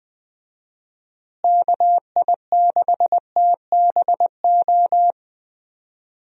Morse code, heard as a keyed tone, about 19 wpm, 700 Hz. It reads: KI6TBO
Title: USA Call Signs - 6 Characters - Repeated 3x - 20wpm